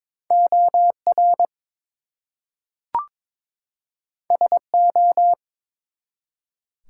Morse code, heard as a keyed tone, about 22 wpm, 700 Hz. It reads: OR SO
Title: Two Letter Words - 22wpm